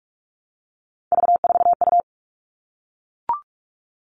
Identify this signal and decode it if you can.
Morse code, heard as a keyed tone, about 44 wpm, 700 Hz. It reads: V4U
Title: All Cut Numbers - 3-Digit 45wpm